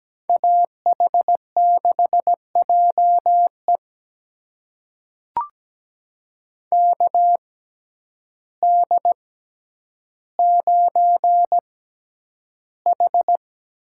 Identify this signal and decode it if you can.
Morse code, heard as a keyed tone, about 17 wpm, spaced as for 9 wpm, 700 Hz. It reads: AH6JE KD9H
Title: USA Call Signs - 5 Characters - 6x Character Spacing - 17wpm